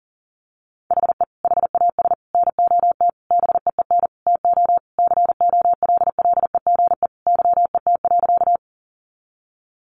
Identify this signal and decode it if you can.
Morse code, heard as a keyed tone, about 40 wpm, 700 Hz. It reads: HE HAS NOT BEEN TO COLLEGE YET.
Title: Sentences from Top 1300 Words 40wpm